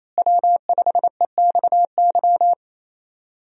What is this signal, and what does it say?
Morse code, heard as a keyed tone, about 27 wpm, 700 Hz. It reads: W5EXY
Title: USA Call Signs - 5 Characters - Repeated 2x - 28wpm